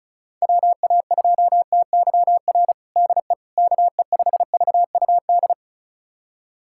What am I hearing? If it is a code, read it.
Morse code, heard as a keyed tone, about 34 wpm, 700 Hz. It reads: WA2TYR DE KE5VUD
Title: USA Callsign DE Callsign - 6 Characters - 35wpm